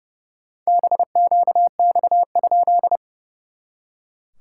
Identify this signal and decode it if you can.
Morse code, heard as a keyed tone, about 30 wpm, 690 Hz. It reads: BQX?
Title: SOTA QSO Practice 30wpm